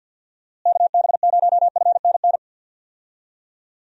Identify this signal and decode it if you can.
Morse code, heard as a keyed tone, about 47 wpm, 700 Hz. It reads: KB0FNN
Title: USA Call Signs - 6 Characters - Repeated 2x - 50wpm